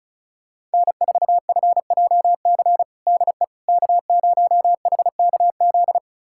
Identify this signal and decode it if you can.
Morse code, heard as a keyed tone, about 36 wpm, 700 Hz. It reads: N4FJC DE K0HKZ
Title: USA Callsign DE Callsign - 5 Characters - 35wpm